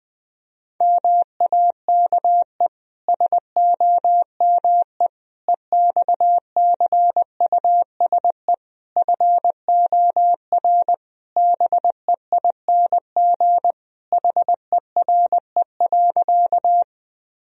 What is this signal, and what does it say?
Morse code, heard as a keyed tone, about 21 wpm, 700 Hz. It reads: MAKE SOME EXCUSE FOR BEING HERE.
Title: Sentences from Top 1000 Words - Review Part 3 - 20wpm